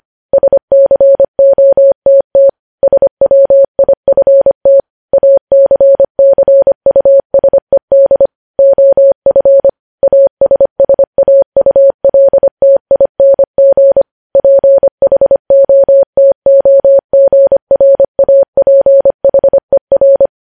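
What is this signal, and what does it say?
Morse code, heard as a keyed tone, about 25 wpm, 550 Hz. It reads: SCOTT SWIFT ACCUSED OF ASSAULTING PHOTOGRAPHER
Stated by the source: Morse code news headlines at 25 wpm.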